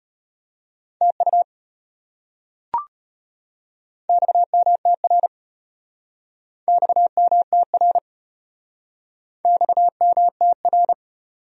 Morse code, heard as a keyed tone, about 33 wpm, 700 Hz. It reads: TU XMTR XMTR XMTR